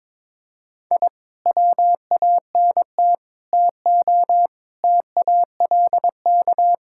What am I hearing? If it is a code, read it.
Morse code, heard as a keyed tone, about 22 wpm, 700 Hz. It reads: I WANT TO TALK
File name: practice / 22wpm / Sentences with 4-Word Phrases - 22wpm